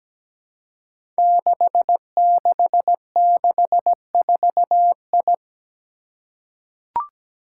Morse code, heard as a keyed tone, about 17 wpm, 700 Hz. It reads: 6664I
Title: Five Challenging Letter-Number Combinations 17wpm